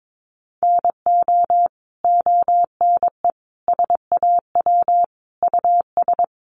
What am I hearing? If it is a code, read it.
Morse code, heard as a keyed tone, about 22 wpm, 700 Hz. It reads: NO ONE SAW US
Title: Sentences from Top 1200 Words 22wpm